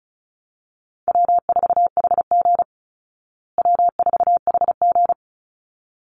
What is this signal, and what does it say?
Morse code, heard as a keyed tone, about 34 wpm, 700 Hz. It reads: W4HG W4HG
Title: Easy Call Signs Repeated 2x - 35wpm